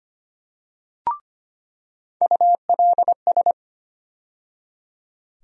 Morse code, heard as a keyed tone, about 25 wpm, 700 Hz. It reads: ULS
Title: USA Call Sign Postfixes - 3 Letters 25wpm